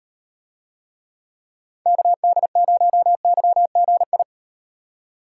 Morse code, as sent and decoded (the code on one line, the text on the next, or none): -.- -.. ----- -.-- --. ..
KD0YGI